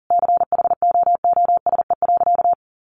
Morse code, heard as a keyed tone, about 40 wpm, 700 Hz. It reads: CHOOSE.
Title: Sentences from Top 700 Words 40wpm